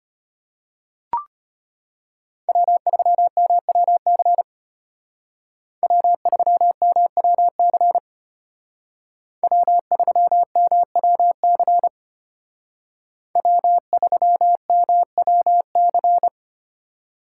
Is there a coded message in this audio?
Yes: Morse code from 2.5 to 16.3 s, heard as a keyed tone, about 33 wpm, 700 Hz. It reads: W3MWC W3MWC W3MWC W3MWC